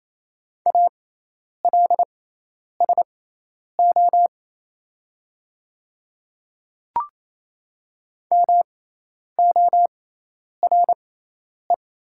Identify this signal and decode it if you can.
Morse code, heard as a keyed tone, about 28 wpm, spaced as for 10 wpm, 700 Hz. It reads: ALSO MORE
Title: 6x Character Spacing - Top 200 Words - 28wpm